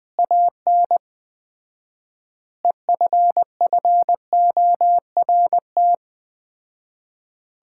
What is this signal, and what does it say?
Morse code, heard as a keyed tone, about 19 wpm, 700 Hz. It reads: AN EFFORT
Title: 4x Word-Spacing - Sets of 4 Words - 20wpm